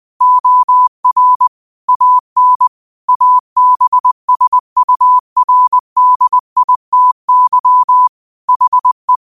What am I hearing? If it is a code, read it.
Morse code, heard as a keyed tone, about 20 wpm, 1000 Hz. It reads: OR AN ABSURDITY HE
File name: 20wpm